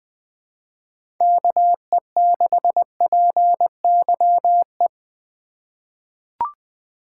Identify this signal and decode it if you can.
Morse code, heard as a keyed tone, about 21 wpm, 700 Hz. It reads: KE6PYE
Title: USA Call Signs - 6 Characters - 2x Character Spacing - 20wpm